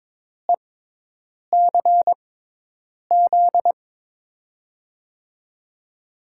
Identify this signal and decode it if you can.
Morse code, heard as a keyed tone, about 22 wpm, spaced as for 8 wpm, 700 Hz. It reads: ECZ
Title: USA Call Signs - 6x Character Spacing - 22wpm